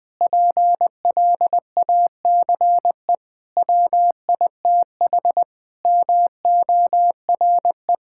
Morse code, heard as a keyed tone, about 20 wpm, 700 Hz. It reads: PLACE WITH MORE
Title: Sentences with 7-Word Phrases - 20wpm